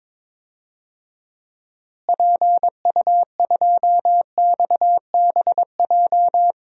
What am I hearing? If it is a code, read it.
Morse code, heard as a keyed tone, about 23 wpm, 700 Hz. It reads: PU2XBJ
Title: DXCC-100 Call Signs - 6 Characters - Repeated 2x - 22wpm